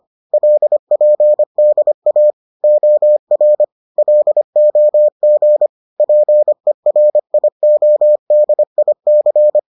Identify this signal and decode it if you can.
Morse code, heard as a keyed tone, about 25 wpm, 600 Hz. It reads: LPDA OR LOG PERIODIC